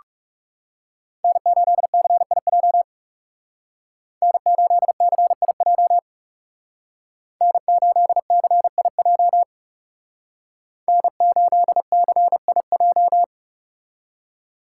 Morse code, heard as a keyed tone, about 36 wpm, 700 Hz. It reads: N8CIJ N8CIJ N8CIJ N8CIJ
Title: Speed Racing - USA 1x3 Call Signs 30wpm